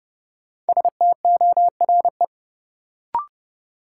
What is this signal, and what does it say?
Morse code, heard as a keyed tone, about 30 wpm, 700 Hz. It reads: STORE
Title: Sentences from Top 1000 Words - Review Part 5 - New Word Drill - 30wpm